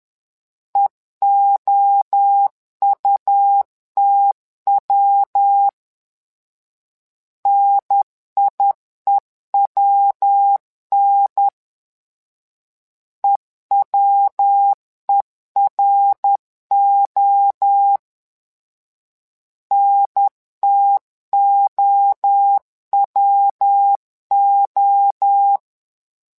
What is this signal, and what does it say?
Morse code, heard as a keyed tone, about 10 wpm, 800 Hz. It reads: EOUTW NIEWN EWERO NTOWO
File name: LEZIONI / 3 MWRO / 3 MWRO 10wpm 2148